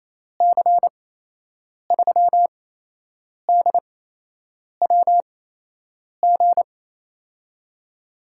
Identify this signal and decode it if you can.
Morse code, heard as a keyed tone, about 28 wpm, spaced as for 8 wpm, 700 Hz. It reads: C3DWG
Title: USA Call Signs - 6 Characters - 8x Character Spacing - 28wpm